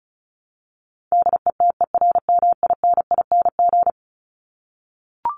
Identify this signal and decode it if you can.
Morse code, heard as a keyed tone, about 34 wpm, 700 Hz. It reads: DETERMINING
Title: ING Suffix - Top 100 Words with Base Words 35wpm